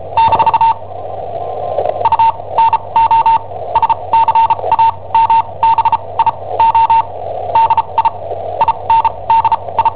Morse code, heard as a keyed tone, about 37 wpm, spaced as for 25 wpm, 910 Hz. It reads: - UNO SCAMBIO DI INDI
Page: Cassetta esercizi preparata da I7OHP per IW5ELP